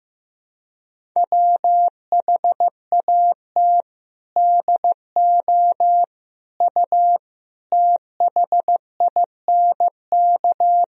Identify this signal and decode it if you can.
Morse code, heard as a keyed tone, about 16 wpm, 700 Hz. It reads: WHAT DO U THINK
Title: Sentences from Top 300 Words and Abbreviations 15wpm